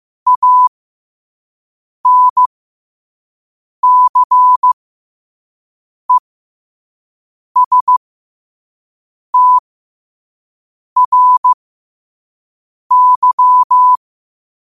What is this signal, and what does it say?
Morse code, heard as a keyed tone, about 15 wpm, spaced as for 5 wpm, 1000 Hz. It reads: ANCESTRY
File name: qotdfiles / 5wpm